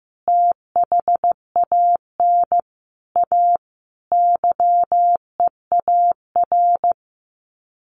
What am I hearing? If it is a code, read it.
Morse code, heard as a keyed tone, about 15 wpm, 700 Hz. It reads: THAN A YEAR
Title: Sets of 4 Words 15wpm